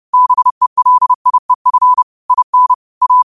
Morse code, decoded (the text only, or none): BELIEF IN A